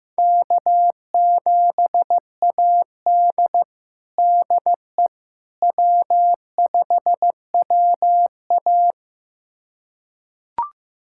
Morse code, heard as a keyed tone, about 15 wpm, 700 Hz. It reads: K7AD DE W5WA
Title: USA Callsign DE Callsign - 4 Characters - 15wpm